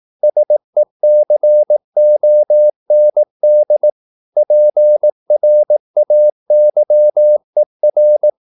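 Morse code, heard as a keyed tone, about 18 wpm, 600 Hz. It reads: SECOND PRAYER
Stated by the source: Morse practice at 18 wpm, English text, from CW med Gnister.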